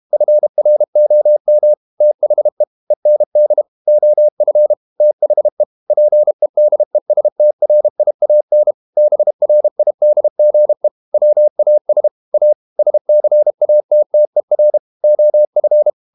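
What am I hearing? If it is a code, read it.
Morse code, heard as a keyed tone, about 32 wpm, 600 Hz. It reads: FROM THE END OF THE PEDESTRIAN BRIDGE WAS A SCATTER OF
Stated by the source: Morse code practice at 32 wpm from CW med Gnister, reading Ken Follett's novel Never in English.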